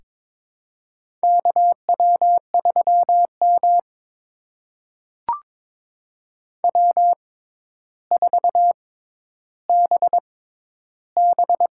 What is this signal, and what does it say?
Morse code, heard as a keyed tone, about 22 wpm, spaced as for 11 wpm, 700 Hz. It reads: KW3M W4BB
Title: USA Call Signs - 4 Characters - 6x Character Spacing - 22wpm